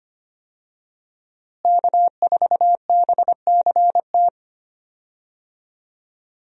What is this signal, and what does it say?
Morse code, heard as a keyed tone, about 25 wpm, 700 Hz. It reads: K4BCT
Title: USA Call Signs - 5 Characters - Repeated 3x - 25wpm